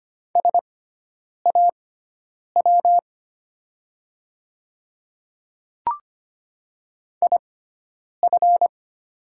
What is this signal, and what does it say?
Morse code, heard as a keyed tone, about 25 wpm, spaced as for 9 wpm, 700 Hz. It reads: SAW IF
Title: 6x Character Spacing - Top 300 Words - 25wpm